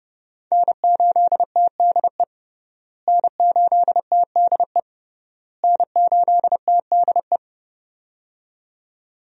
Morse code, heard as a keyed tone, about 30 wpm, 700 Hz. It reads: N8TDE N8TDE N8TDE
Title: USA Call Signs - 5 Characters - Repeated 3x - 30wpm